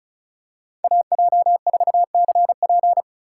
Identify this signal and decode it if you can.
Morse code, heard as a keyed tone, about 34 wpm, 700 Hz. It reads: AJ4CP
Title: USA Call Signs - 5 Characters - 4x Character Spacing - 35wpm